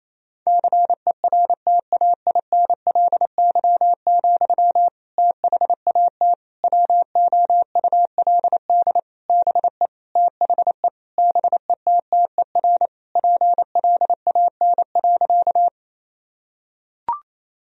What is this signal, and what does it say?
Morse code, heard as a keyed tone, about 28 wpm, 690 Hz. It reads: CERTAINLY, THAT WOULD BE THE BETTER PLAN.
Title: Sentences from Top 1000 Words - Review Part 2 - 28wpm